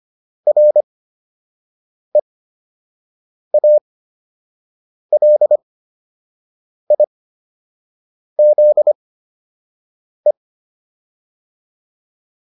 Morse code, heard as a keyed tone, about 25 wpm, spaced as for 6 wpm, 600 Hz. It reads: REALIZE